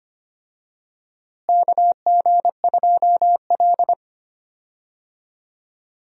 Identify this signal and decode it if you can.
Morse code, heard as a keyed tone, about 25 wpm, 700 Hz. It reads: KG2L
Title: USA Call Signs - 4 Characters - Repeated 2x - 25wpm